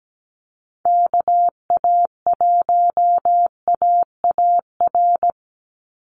Morse code, heard as a keyed tone, about 17 wpm, 700 Hz. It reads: KA1AAR
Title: USA Call Signs 17wpm